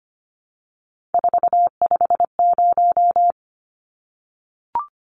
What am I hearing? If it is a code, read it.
Morse code, heard as a keyed tone, about 25 wpm, 700 Hz. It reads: 450
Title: Numbers - 3-Digit 25wpm